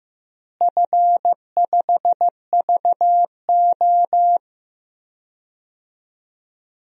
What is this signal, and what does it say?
Morse code, heard as a keyed tone, about 15 wpm, 700 Hz. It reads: F5VO
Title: DXCC-100 Call Signs - 4 Characters - Repeated 3x - 15wpm